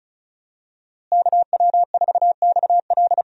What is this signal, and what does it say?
Morse code, heard as a keyed tone, about 36 wpm, 700 Hz. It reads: KW4XL
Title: USA Call Signs - 5 Characters - Repeated 3x - 35wpm